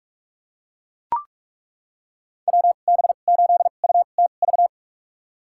Morse code, heard as a keyed tone, about 42 wpm, spaced as for 31 wpm, 700 Hz. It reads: WB8UTV
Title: USA Call Signs - 6 Characters - 2x Character Spacing - 45wpm